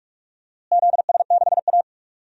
Morse code, heard as a keyed tone, about 43 wpm, 700 Hz. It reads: ZS6A
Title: DXCC-100 Call Signs - 4 Characters - Repeated 3x - 45wpm